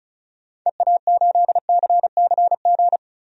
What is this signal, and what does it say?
Morse code, heard as a keyed tone, about 36 wpm, 700 Hz. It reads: EA8CCG
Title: DXCC-100 Call Signs - 6 Characters - 35wpm